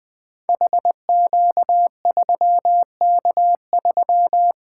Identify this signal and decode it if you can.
Morse code, heard as a keyed tone, about 20 wpm, 700 Hz. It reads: HQ3K3
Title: Code Groups - 5 Letter:Numbers - 20wpm